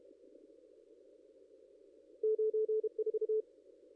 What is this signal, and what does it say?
Morse code, heard as a keyed tone, about 33 wpm, 430 Hz. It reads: 94
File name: contests / ss-cw-k5tr-op-n5rz -2200